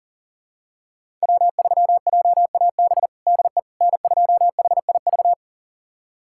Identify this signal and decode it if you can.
Morse code, heard as a keyed tone, about 37 wpm, 700 Hz. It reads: W3JAB DE N2HIV
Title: USA Callsign DE Callsign - 5 Characters - 40wpm